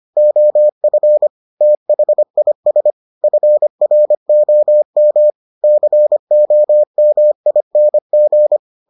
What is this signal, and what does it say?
Morse code, heard as a keyed tone, about 25 wpm, 600 Hz. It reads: OF THIS FROM COMING